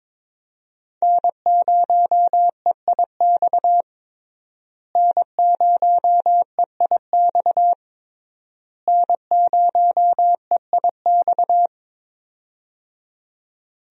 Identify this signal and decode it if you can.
Morse code, heard as a keyed tone, about 22 wpm, 700 Hz. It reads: N0EIX N0EIX N0EIX
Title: USA Call Signs - 5 Characters - Repeated 3x - 22wpm